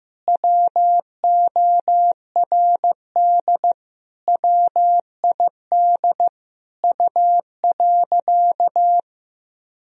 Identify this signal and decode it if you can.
Morse code, heard as a keyed tone, about 16 wpm, 700 Hz. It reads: WORD WID U.
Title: Sentences from Top 300 Words and Abbreviations 15wpm